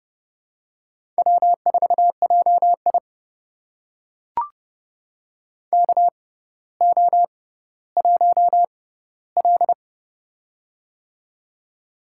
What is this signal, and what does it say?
Morse code, heard as a keyed tone, about 30 wpm, spaced as for 15 wpm, 700 Hz. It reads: W4JI KO1L
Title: USA Call Signs - 4 Characters - 6x Character Spacing - 30wpm